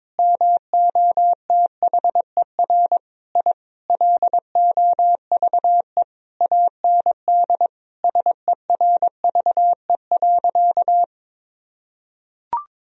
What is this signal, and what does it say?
Morse code, heard as a keyed tone, about 22 wpm, 700 Hz. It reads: MOTHER I LOVE AND SERVE.
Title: Sentences from Top 1000 Words - Review Part 2 - 22wpm